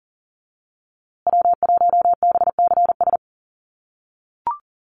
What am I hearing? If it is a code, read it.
Morse code, heard as a keyed tone, about 37 wpm, 700 Hz. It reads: W1BCS
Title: USA Call Signs Repeated 3x 40wpm